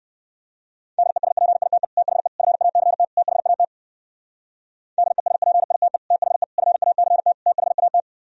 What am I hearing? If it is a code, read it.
Morse code, heard as a keyed tone, about 58 wpm, 700 Hz. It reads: DESPITE THE FACT THAT DESPITE THE FACT THAT